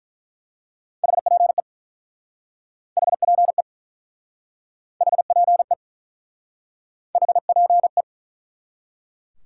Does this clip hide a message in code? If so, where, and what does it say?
Yes: Morse code from 1.0 to 8.0 s, heard as a keyed tone, about 47 wpm, 700 Hz. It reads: HPE HPE HPE HPE